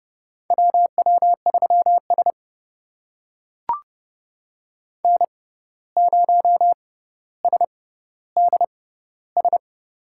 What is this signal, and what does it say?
Morse code, heard as a keyed tone, about 31 wpm, spaced as for 15 wpm, 700 Hz. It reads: WW3S N0SDS
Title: USA Call Signs - 6x Character Spacing - 30wpm